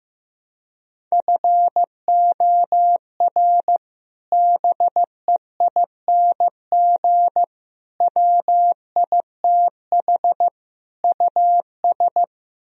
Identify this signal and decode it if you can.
Morse code, heard as a keyed tone, about 15 wpm, 700 Hz. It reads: FOR BEING WITH US
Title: Speed-Racing - Sets of 4 Words 15wpm